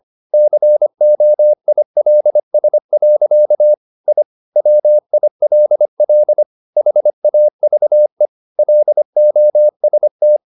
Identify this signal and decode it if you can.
Morse code, heard as a keyed tone, about 25 wpm, 600 Hz. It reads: COILS. I WILL HAVE LOST
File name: the old man and the sea part 1 25wpm